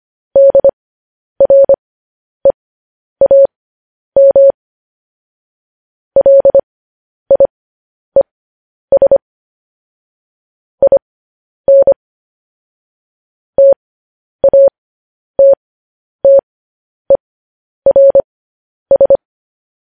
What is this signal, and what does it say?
Morse code, heard as a keyed tone, about 25 wpm, spaced as for 10 wpm, 550 Hz. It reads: DREAM LIES IN TATTERS
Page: news-headlines-in-morse-code-at-10-wpm